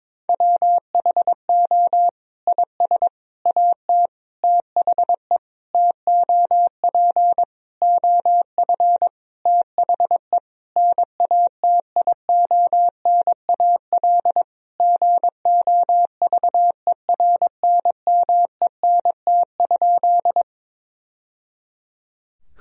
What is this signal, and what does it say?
Morse code, heard as a keyed tone, about 22 wpm, 700 Hz. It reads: WHO IS AT THE TOP OF THE NATIONAL GOVERNMENT?
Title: Sentences from Top 1000 Words - Review Part 4 - 22wpm